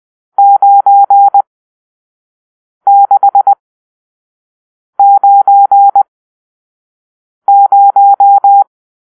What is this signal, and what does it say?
Morse code, heard as a keyed tone, about 20 wpm, spaced as for 5 wpm, 800 Hz. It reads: 9690